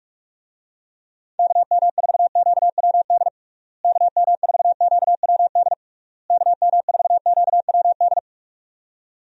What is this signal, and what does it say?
Morse code, heard as a keyed tone, about 46 wpm, 700 Hz. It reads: KM4QWD KM4QWD KM4QWD
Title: USA Call Signs - 6 Characters - Repeated 3x - 45wpm